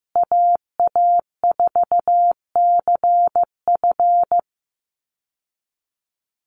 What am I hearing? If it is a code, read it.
Morse code, heard as a keyed tone, about 15 wpm, 700 Hz. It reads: AA4CF
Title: Call Signs Repeated 2x - 15wpm